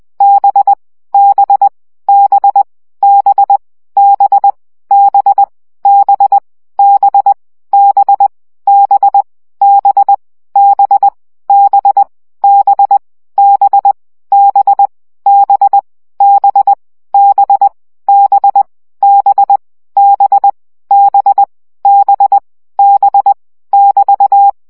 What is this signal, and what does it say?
Morse code, heard as a keyed tone, about 20 wpm, 800 Hz. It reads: B B B B B B B B B B B B B B B B B B B B B B B B B <BT>
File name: CW gyakorlas b 17 baud